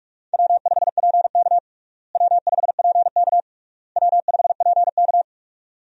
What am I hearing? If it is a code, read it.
Morse code, heard as a keyed tone, about 42 wpm, 700 Hz. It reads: W5PK W5PK W5PK
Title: USA Call Signs - 4 Characters - Repeated 3x - 45wpm